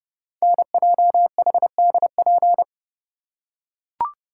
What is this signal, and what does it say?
Morse code, heard as a keyed tone, about 30 wpm, 700 Hz. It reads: NJHDP
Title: Code Groups - 5 Letters - 30wpm